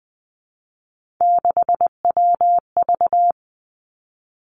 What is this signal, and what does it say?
Morse code, heard as a keyed tone, about 20 wpm, 700 Hz. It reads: 6WV
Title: Three Challenging Letter-Number Combinations 20wpm